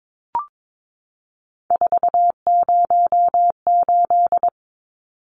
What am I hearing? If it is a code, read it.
Morse code, heard as a keyed tone, about 22 wpm, 700 Hz. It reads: 408
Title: Numbers - 3-Digit 22wpm